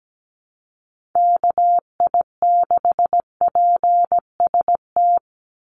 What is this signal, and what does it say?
Morse code, heard as a keyed tone, about 17 wpm, 700 Hz. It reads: KI6PST
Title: USA Call Signs 17wpm